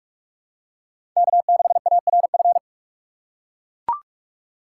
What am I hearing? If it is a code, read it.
Morse code, heard as a keyed tone, about 42 wpm, 700 Hz. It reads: K6ARF
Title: USA Call Signs - 5 Characters - 3x Character Spacing - 45wpm